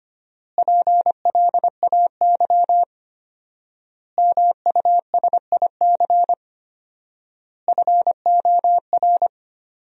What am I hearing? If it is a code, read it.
Morse code, heard as a keyed tone, about 25 wpm, 700 Hz. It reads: PLAY MUSIC FOR
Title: 4x Word-Spacing - Sentences from Top 300 Words - 25wpm